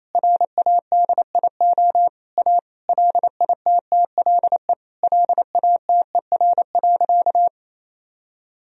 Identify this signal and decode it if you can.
Morse code, heard as a keyed tone, about 28 wpm, 700 Hz. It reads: RADIO A LITTLE LATER.
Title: Sentences from Top 1000 Words - Review Part 3 - 28wpm